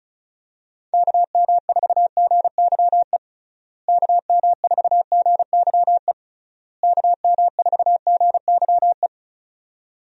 Morse code, heard as a keyed tone, about 36 wpm, 700 Hz. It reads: KM4GYE KM4GYE KM4GYE
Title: USA Call Signs - 6 Characters - Repeated 3x - 35wpm